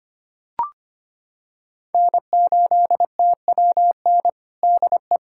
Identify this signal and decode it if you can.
Morse code, heard as a keyed tone, about 25 wpm, 700 Hz. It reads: N8TWN DE
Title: USA Callsign DE Callsign 25wpm